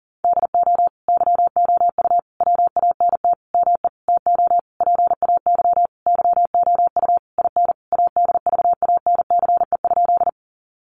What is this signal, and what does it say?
Morse code, heard as a keyed tone, about 37 wpm, 700 Hz. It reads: DO YOU WANT ME TO PAY YOU IN ADVANCE?
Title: Sentences from Top 1400 Words 40wpm